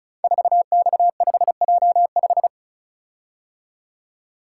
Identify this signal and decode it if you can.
Morse code, heard as a keyed tone, about 36 wpm, 700 Hz. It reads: 4X5J5
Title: Code Groups - 5 Letter:Numbers - 35wpm